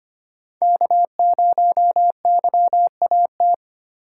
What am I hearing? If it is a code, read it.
Morse code, heard as a keyed tone, about 25 wpm, 700 Hz. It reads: K0YAT
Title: USA Call Signs - 5 Characters - 4x Character Spacing - 25wpm